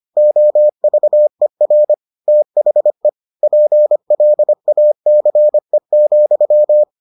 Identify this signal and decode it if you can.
Morse code, heard as a keyed tone, about 25 wpm, 600 Hz. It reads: OVER THE PLACE,